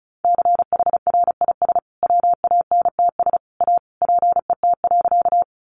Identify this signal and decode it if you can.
Morse code, heard as a keyed tone, about 34 wpm, 700 Hz. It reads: CHRIS WANTS A PET.
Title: Additional Easy Phrases 35wpm